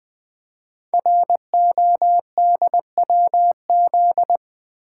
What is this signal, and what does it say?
Morse code, heard as a keyed tone, about 20 wpm, 700 Hz. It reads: RODWZ
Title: Code Groups - 5 Letters - 20wpm